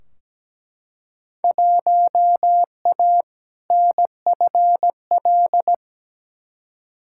Morse code, heard as a keyed tone, about 17 wpm, 700 Hz. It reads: 1A NFL
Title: Field Day Exchanges - 17wpm